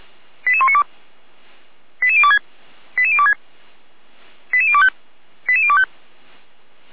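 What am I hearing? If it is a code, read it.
Five-tone selective call (ZVEI), heard as quick runs of five tones, tones of about 80 ms. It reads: 88192 88026 88026 88025 88025
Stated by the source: ZVEI five-tone call received with an SDR.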